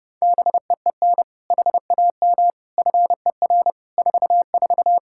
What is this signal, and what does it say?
Morse code, heard as a keyed tone, about 30 wpm, 700 Hz. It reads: BEEN HAM FER 44
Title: Practice QSO - Exchange 3 - More Info - 30wpm